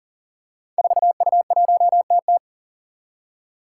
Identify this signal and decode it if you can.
Morse code, heard as a keyed tone, about 40 wpm, 700 Hz. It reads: 4U1TT
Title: Five Challenging Letter-Number Combinations 40wpm